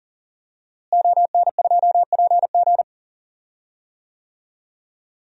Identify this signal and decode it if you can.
Morse code, heard as a keyed tone, about 40 wpm, 700 Hz. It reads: ON2PG
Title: DXCC-100 Call Signs - 5 Characters - Repeated 3x - 40wpm